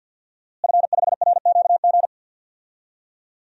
Morse code, heard as a keyed tone, about 51 wpm, 700 Hz. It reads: F5RQG
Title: DXCC-100 Call Signs - 5 Characters - Repeated 3x - 50wpm